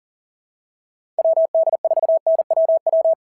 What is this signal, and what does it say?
Morse code, heard as a keyed tone, about 40 wpm, 690 Hz. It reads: WD4NWW
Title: SOTA QSO Practice 40wpm